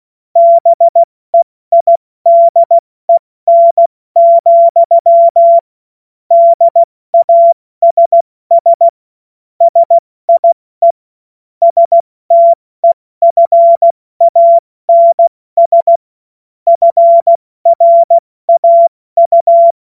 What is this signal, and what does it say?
Morse code, heard as a keyed tone, about 16 wpm, 680 Hz. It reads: BEIDEN, DASS SIE STEFANS FRAU
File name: Morsekurs / Witze / Witz 30 Januar 2024 70 F80